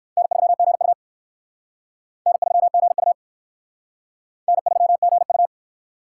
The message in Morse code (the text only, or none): N3GU N3GU N3GU